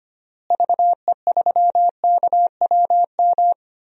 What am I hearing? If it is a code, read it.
Morse code, heard as a keyed tone, about 25 wpm, 700 Hz. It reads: VE3KWM
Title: DXCC-100 Call Signs - 6 Characters - Repeated 3x - 25wpm